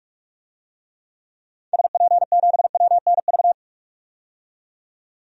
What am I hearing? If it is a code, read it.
Morse code, heard as a keyed tone, about 45 wpm, 700 Hz. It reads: SP7WNV
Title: DXCC-100 Call Signs - 6 Characters - 45wpm